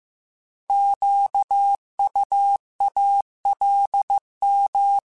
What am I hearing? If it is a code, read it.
Morse code, heard as a keyed tone, about 15 wpm, 800 Hz. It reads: QUALM